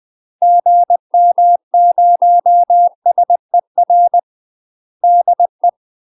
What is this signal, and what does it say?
Morse code, heard as a keyed tone, about 20 wpm, 700 Hz. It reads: GM0SER DE
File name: UK MASTER QSOs TRAINING / 20WPM / UK MASTER QSOs 0067